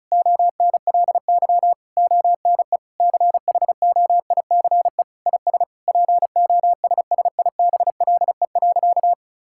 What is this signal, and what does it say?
Morse code, heard as a keyed tone, about 34 wpm, 700 Hz. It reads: ONLY ONE CHOICE IS POSSIBLE.
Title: Sentences from Top 1000 Words - Review Part 6 - 35wpm